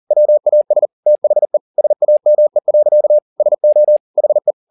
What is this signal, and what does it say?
Morse code, heard as a keyed tone, about 40 wpm, 600 Hz. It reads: WAS THE SAME. SO HE
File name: the old man and the sea 2 40wpm